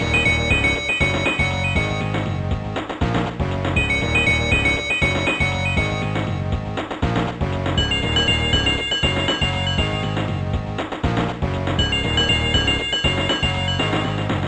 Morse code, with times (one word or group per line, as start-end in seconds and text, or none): none